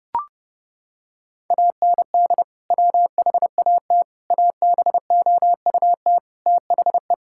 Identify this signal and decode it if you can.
Morse code, heard as a keyed tone, about 30 wpm, 700 Hz. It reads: AND WHAT ABOUT THE
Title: Sentences from Top 1000 Words - Review Part 5 - 30wpm